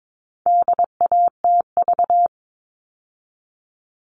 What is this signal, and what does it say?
Morse code, heard as a keyed tone, about 22 wpm, 700 Hz. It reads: DATV
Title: All Cut Numbers - 4-Digit 22wpm